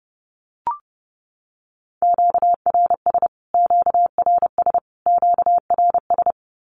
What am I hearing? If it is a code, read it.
Morse code, heard as a keyed tone, about 30 wpm, 700 Hz. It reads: QRS QRS QRS
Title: Q-Codes Repeated 3x 30wpm